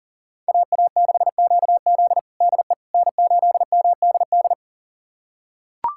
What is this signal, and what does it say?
Morse code, heard as a keyed tone, about 40 wpm, 700 Hz. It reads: AA6QZ DE N8MDD
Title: USA Callsign DE Callsign - 5 Characters - 40wpm